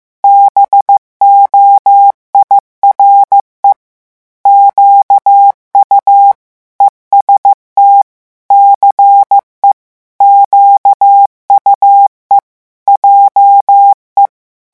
Morse code, heard as a keyed tone, about 15 wpm, 800 Hz. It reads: BOIRE QU EST CE QUE JE
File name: morse mp3 / revision9 citations humoristiques vitesse 14 mots minute